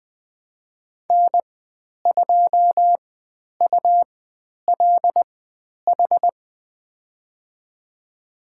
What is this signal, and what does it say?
Morse code, heard as a keyed tone, about 20 wpm, spaced as for 10 wpm, 700 Hz. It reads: N2ULH
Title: 109 Lesson 18g - Letter m - Call Signs 20wpm